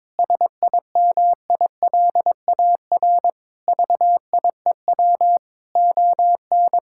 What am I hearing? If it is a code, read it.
Morse code, heard as a keyed tone, about 22 wpm, 700 Hz. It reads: SIMILAR VIEW ON
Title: Sentences from Top 1000 Words - Review Part 6 - 22wpm